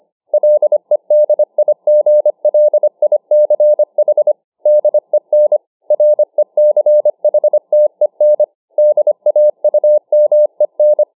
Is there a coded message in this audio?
Yes: Morse code from 0.3 to 11.0 s, heard as a keyed tone, about 25 wpm, 600 Hz. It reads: LEDIGLICH DEN RECHTEN DAUMEN